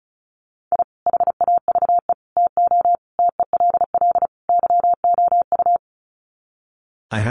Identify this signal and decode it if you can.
Morse code, heard as a keyed tone, about 34 wpm, 700 Hz. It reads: I HAVE TO TELL YOU
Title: Sets of 5 Words 35wpm